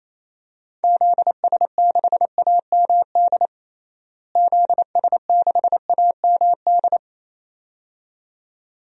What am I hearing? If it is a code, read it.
Morse code, heard as a keyed tone, about 28 wpm, 700 Hz. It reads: ZS6AMD ZS6AMD
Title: DXCC-100 Call Signs - 6 Characters - Repeated 2x - 28wpm